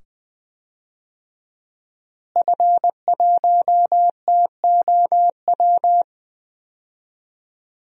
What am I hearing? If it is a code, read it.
Morse code, heard as a keyed tone, about 20 wpm, 700 Hz. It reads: F1TOW
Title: DXCC-100 Call Signs - 5 Characters - Repeated 2x - 20wpm